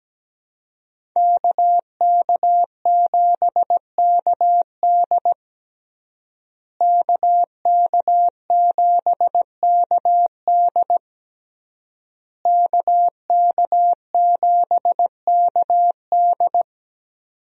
Morse code, heard as a keyed tone, about 17 wpm, 700 Hz. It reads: KK7KD KK7KD KK7KD
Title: USA Call Signs - 5 Characters - Repeated 3x - 17wpm